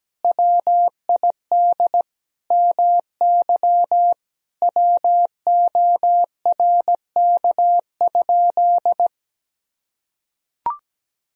Morse code, heard as a keyed tone, about 17 wpm, 700 Hz. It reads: WID MY WORK?
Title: Sentences from Top 500 Words and Abbreviations 17wpm